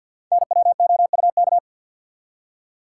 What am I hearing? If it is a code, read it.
Morse code, heard as a keyed tone, about 49 wpm, 700 Hz. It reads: NWOUK
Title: Code Groups - 5 Letters - 50wpm